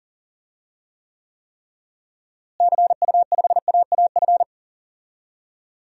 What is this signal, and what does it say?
Morse code, heard as a keyed tone, about 40 wpm, 700 Hz. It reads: CU5AAF
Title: DXCC-100 Call Signs - 6 Characters - Repeated 3x - 40wpm